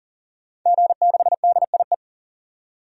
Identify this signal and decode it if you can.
Morse code, heard as a keyed tone, about 40 wpm, 700 Hz. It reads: G6DIE
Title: DXCC-100 Call Signs - 5 Characters - 40wpm